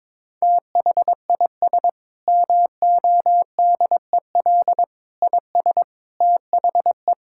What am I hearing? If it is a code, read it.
Morse code, heard as a keyed tone, about 22 wpm, 700 Hz. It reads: THIS MODEL IS THE
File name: practice / 22wpm / Sentences with 8-Word Phrases - 22wpm